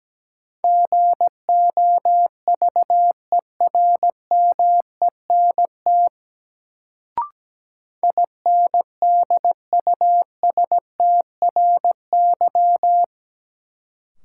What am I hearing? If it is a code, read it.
Morse code, heard as a keyed tone, about 17 wpm, 700 Hz. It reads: GOVERMENT INDUSTRY
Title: Sentences from Top 1300 Words - New Word Drill Alt - 17wpm